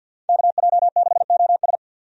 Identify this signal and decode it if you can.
Morse code, heard as a keyed tone, about 50 wpm, 700 Hz. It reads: KJ6OS
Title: USA Call Signs - 5 Characters - 50wpm